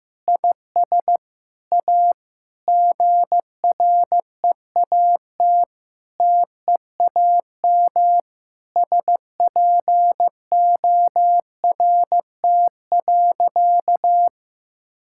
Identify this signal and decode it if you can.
Morse code, heard as a keyed tone, about 15 wpm, 700 Hz. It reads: IS A GREAT TEAM SPORT.
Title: Sentences from Top 1600 Words 15wpm